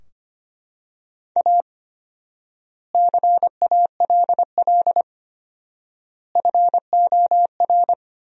Morse code, heard as a keyed tone, about 25 wpm, 700 Hz. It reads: A CALL FOR
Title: 4x Word-Spacing - Sentences from Top 300 Words - 25wpm